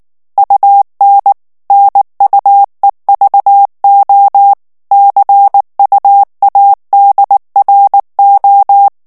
Morse code, heard as a keyed tone, about 19 wpm, 800 Hz. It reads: UN NUEVO CUADRO
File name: morse mp3 / ESPAGNOL TEXTE EN CLAIR 2 VITESSE 18 MOTS MINUTES